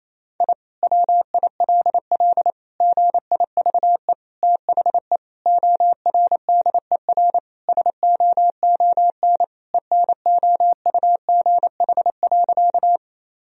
Morse code, heard as a keyed tone, about 27 wpm, 700 Hz. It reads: I WILL GIVE THE ORDER SOON ENOUGH.
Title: Sentences from Top 1000 Words - Review Part 2 - 28wpm